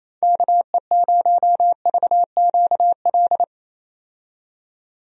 Morse code, heard as a keyed tone, about 28 wpm, 700 Hz. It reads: KE0VQL
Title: USA Call Signs - 6 Characters - 28wpm